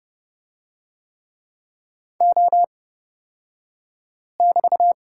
Morse code, heard as a keyed tone, about 30 wpm, spaced as for 5 wpm, 700 Hz. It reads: O<BT>
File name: Lesson 36d - BT - Two Characters 30wpm